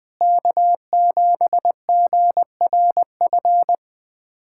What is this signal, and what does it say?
Morse code, heard as a keyed tone, about 20 wpm, 700 Hz. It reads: K7GRF
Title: USA Callsign DE Callsign - 5 Characters - 20wpm